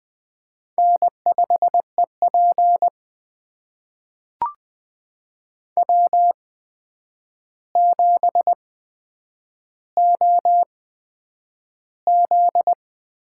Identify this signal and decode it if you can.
Morse code, heard as a keyed tone, about 20 wpm, spaced as for 9 wpm, 700 Hz. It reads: N5EP W7OZ
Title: USA Call Signs - 4 Characters - 8x Character Spacing - 20wpm